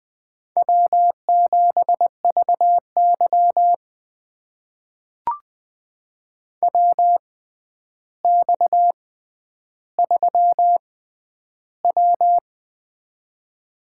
Morse code, heard as a keyed tone, about 20 wpm, spaced as for 10 wpm, 700 Hz. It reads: W7VY WX3W
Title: USA Call Signs - 4 Characters - 6x Character Spacing - 20wpm